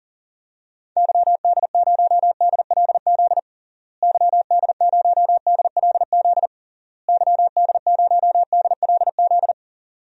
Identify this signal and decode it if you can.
Morse code, heard as a keyed tone, about 40 wpm, 700 Hz. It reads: YD0DLZ YD0DLZ YD0DLZ
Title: DXCC-100 Call Signs - 6 Characters - Repeated 3x - 40wpm